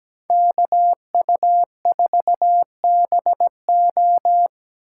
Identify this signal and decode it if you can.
Morse code, heard as a keyed tone, about 17 wpm, 700 Hz. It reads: KU4BO
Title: USA Call Signs - 5 Characters - 8x Character Spacing - 17wpm